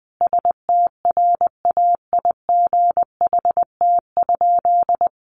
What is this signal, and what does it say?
Morse code, heard as a keyed tone, about 20 wpm, 700 Hz. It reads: STRAIGHT?
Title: Sentences from Top 1000 Words 20wpm